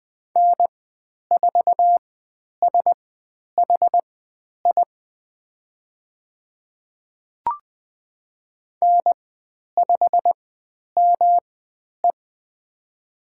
Morse code, heard as a keyed tone, about 20 wpm, spaced as for 10 wpm, 700 Hz. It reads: N4SHI N5ME
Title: 109 Lesson 18g - Letter m - Call Signs 20wpm